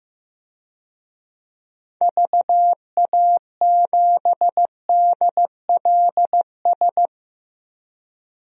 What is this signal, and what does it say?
Morse code, heard as a keyed tone, about 15 wpm, 700 Hz. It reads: VA7DLS
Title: DXCC-100 Call Signs - 6 Characters - 15wpm